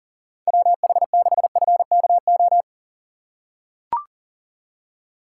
Morse code, heard as a keyed tone, about 40 wpm, 700 Hz. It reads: WH6FKO
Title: USA Call Signs - 6 Characters - 3x Character Spacing - 40wpm